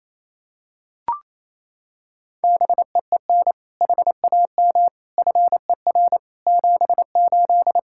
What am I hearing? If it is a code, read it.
Morse code, heard as a keyed tone, about 28 wpm, 700 Hz. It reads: BEEN HAM FER 78
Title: Practice QSO - Exchange 3 - More Info - 28wpm